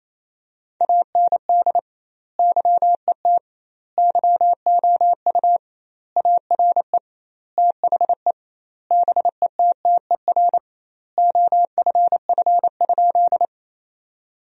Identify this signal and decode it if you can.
Morse code, heard as a keyed tone, about 28 wpm, 700 Hz. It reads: AND YET YOU ARE THE BETTER OFF?
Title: 2x Word-Spacing - Sentences from Top 500 Words - 28wpm